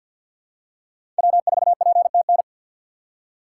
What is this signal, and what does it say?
Morse code, heard as a keyed tone, about 53 wpm, 700 Hz. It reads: W4PTN
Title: USA Call Signs - 6x Character Spacing - 50wpm